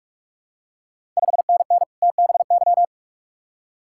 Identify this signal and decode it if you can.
Morse code, heard as a keyed tone, about 44 wpm, 690 Hz. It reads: 5NN TBY
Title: Ontario QSO Party Exchange 45wpm